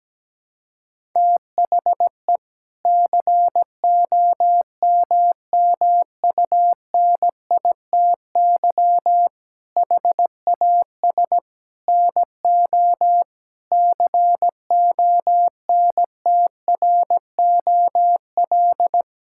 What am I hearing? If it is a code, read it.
Morse code, heard as a keyed tone, about 17 wpm, 700 Hz. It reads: THE COMMUNITY HAS NO CONTROL
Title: Sentences from Top 1000 Words - Review Part 5 - 17wpm